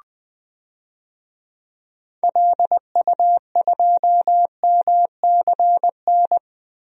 Morse code, heard as a keyed tone, about 20 wpm, 700 Hz. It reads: LU2MCN
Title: DXCC-100 Call Signs - 6 Characters - 20wpm